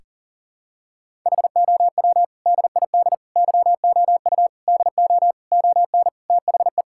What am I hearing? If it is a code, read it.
Morse code, heard as a keyed tone, about 40 wpm, 700 Hz. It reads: HOW DID YOU DO ON THE
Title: Sentences from Top 1000 Words - Review Part 5 - 40wpm